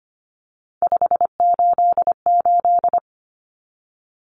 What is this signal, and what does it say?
Morse code, heard as a keyed tone, about 25 wpm, 700 Hz. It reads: 588
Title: Numbers - 3-Digit 25wpm